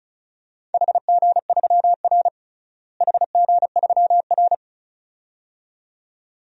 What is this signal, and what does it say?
Morse code, heard as a keyed tone, about 34 wpm, 700 Hz. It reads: HG3R HG3R
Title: DXCC-100 Call Signs - 4 Characters - Repeated 2x - 35wpm